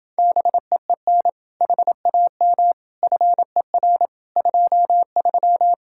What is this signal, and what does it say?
Morse code, heard as a keyed tone, about 29 wpm, 700 Hz. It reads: BEEN HAM FER 23